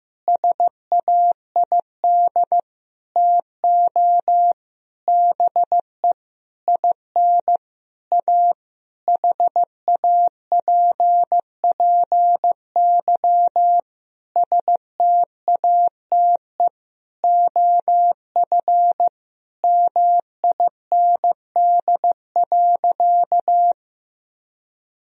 Morse code, heard as a keyed tone, about 15 wpm, 700 Hz. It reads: SAID TO BE IN A HAPPY STATE OF MIND.
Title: Sentences from Top 1000 Words - Review Part 2 - 15wpm